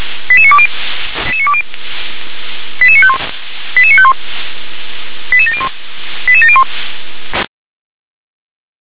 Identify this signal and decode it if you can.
Five-tone selective call (ZVEI), heard as quick runs of five tones, tones of about 70 ms. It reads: 88920 88951 88951 88701 88701